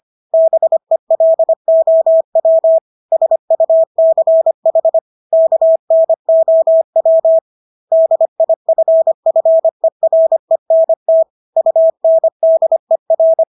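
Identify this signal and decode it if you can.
Morse code, heard as a keyed tone, about 25 wpm, 650 Hz. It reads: BELOW SUCH KNOW DIFFERENT UNDER
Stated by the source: Morse practice of English words at 25 wpm from CW med Gnister.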